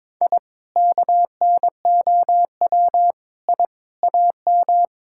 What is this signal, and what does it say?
Morse code, heard as a keyed tone, about 22 wpm, 700 Hz. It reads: I KNOW I AM
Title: Sentences from Top 1000 Words - Review Part 5 - 22wpm